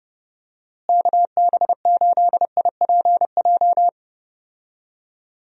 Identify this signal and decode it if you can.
Morse code, heard as a keyed tone, about 30 wpm, 700 Hz. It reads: KB8IPJ
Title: USA Call Signs - 6 Characters - Repeated 3x - 30wpm